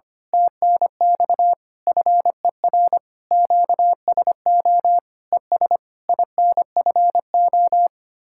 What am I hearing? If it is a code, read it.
Morse code, heard as a keyed tone, about 24 wpm, 700 Hz. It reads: TNX FER QSO ES INFO
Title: Speed-Racing Common QSO Phrases - 25wpm